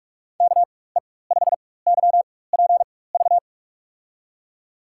Morse code, heard as a keyed tone, about 46 wpm, spaced as for 21 wpm, 700 Hz. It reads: KE5YPV
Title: USA Call Signs - 6 Characters - 4x Character Spacing - 45wpm